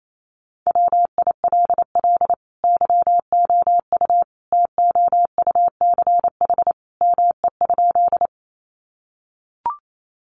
Morse code, heard as a keyed tone, about 28 wpm, 700 Hz. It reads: WILL YOU TOUCH ME?
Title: Sentences from Top 1200 Words 28wpm